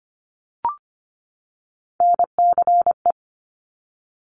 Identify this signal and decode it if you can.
Morse code, heard as a keyed tone, about 25 wpm, 700 Hz. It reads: NCE
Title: Top Three Letter Combinations in Words 25wpm